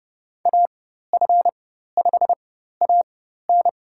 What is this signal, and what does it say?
Morse code, heard as a keyed tone, about 30 wpm, spaced as for 14 wpm, 700 Hz. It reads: AF5AN
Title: USA Call Signs - 5 Characters - 4x Character Spacing - 30wpm